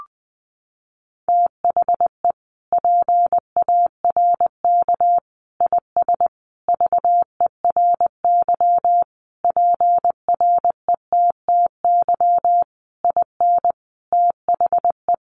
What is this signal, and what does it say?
Morse code, heard as a keyed tone, about 20 wpm, 700 Hz. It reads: THE PARK IS VERY PRETTY IN THE
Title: Sentences from Top 1300 Words 20wpm